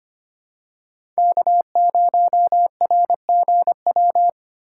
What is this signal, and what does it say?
Morse code, heard as a keyed tone, about 25 wpm, 700 Hz. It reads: K0RGW
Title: USA Call Signs - 5 Characters - 3x Character Spacing - 25wpm